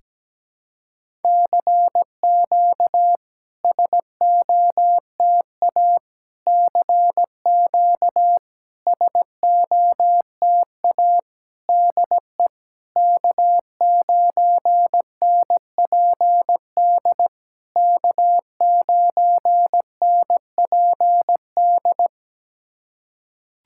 Morse code, heard as a keyed tone, about 17 wpm, 700 Hz. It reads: CQ SOTA CQ SOTA DE K9NPD K9NPD
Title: SOTA QSO Practice 17wpm